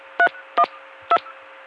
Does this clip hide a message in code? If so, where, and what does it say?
Yes: DTMF from 0.2 to 1.2 s, heard as telephone touch tones, tones of about 70 ms. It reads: 312